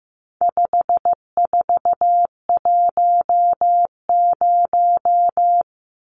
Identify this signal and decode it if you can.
Morse code, heard as a keyed tone, about 15 wpm, 700 Hz. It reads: 5410
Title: Numbers - 4-Digit 15wpm